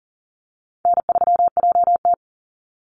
Morse code, heard as a keyed tone, about 40 wpm, 700 Hz. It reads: N3JT
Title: Easy Call Signs Repeated 3x - 40wpm